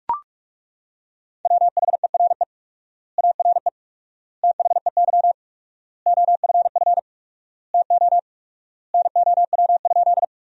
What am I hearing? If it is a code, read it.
Morse code, heard as a keyed tone, about 41 wpm, 700 Hz. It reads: WHERE ARE THEY OFF TO NOW?
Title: 4x Word-Spacing - Sentences from Top 200 Words - 45wpm